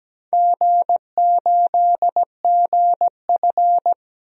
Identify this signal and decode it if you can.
Morse code, heard as a keyed tone, about 17 wpm, 700 Hz. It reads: G8GF
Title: DXCC-100 Call Signs - 4 Characters - 17wpm